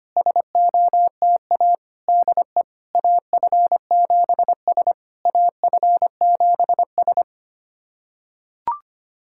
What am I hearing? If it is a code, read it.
Morse code, heard as a keyed tone, about 25 wpm, 700 Hz. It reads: SOTA DE AF7S AF7S
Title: SOTA QSO Practice 25wpm